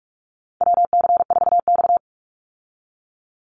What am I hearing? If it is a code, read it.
Morse code, heard as a keyed tone, about 44 wpm, 700 Hz. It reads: WC4X
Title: Call Signs Repeated 2x - 45wpm